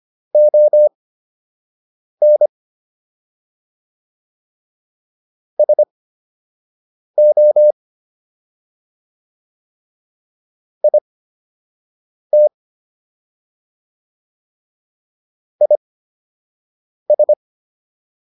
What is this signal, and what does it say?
Morse code, heard as a keyed tone, about 25 wpm, spaced as for 6 wpm, 600 Hz. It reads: ON SO IT IS